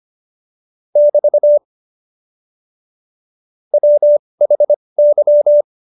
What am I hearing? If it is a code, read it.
Morse code, heard as a keyed tone, about 25 wpm, spaced as for 20 wpm, 590 Hz. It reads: <BT> WHY
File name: Jokes-25-char-20-eff